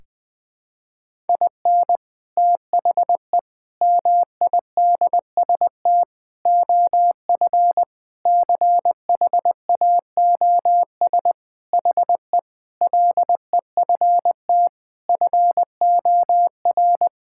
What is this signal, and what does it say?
Morse code, heard as a keyed tone, about 20 wpm, 700 Hz. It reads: IN THE MIDST OF CHAOS HE LEFT FOR
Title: Sentences with 4-Word Phrases - Spoken Beginning - 20wpm